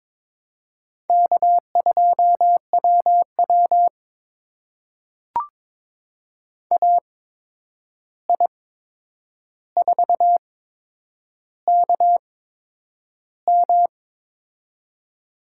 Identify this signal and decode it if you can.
Morse code, heard as a keyed tone, about 23 wpm, spaced as for 9 wpm, 700 Hz. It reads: K2WW AI4KM
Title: USA Call Signs - 8x Character Spacing - 22wpm